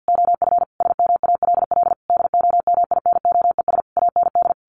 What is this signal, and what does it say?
Morse code, heard as a keyed tone, about 50 wpm, 700 Hz. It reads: OF SMALL DOMINOES AND
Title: A Sound Of Thunder 50wpm